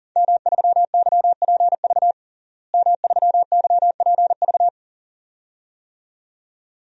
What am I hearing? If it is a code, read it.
Morse code, heard as a keyed tone, about 40 wpm, 700 Hz. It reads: M3YPV M3YPV
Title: DXCC-100 Call Signs - 5 Characters - Repeated 2x - 40wpm